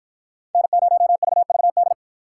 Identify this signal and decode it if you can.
Morse code, heard as a keyed tone, about 53 wpm, 700 Hz. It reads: N0VVD